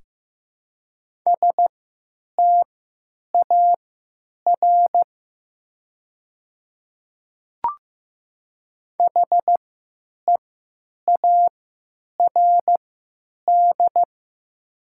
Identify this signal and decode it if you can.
Morse code, heard as a keyed tone, about 15 wpm, spaced as for 9 wpm, 700 Hz. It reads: STAR HEARD
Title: 3x Character Spacing - Top 500 Words - 15wpm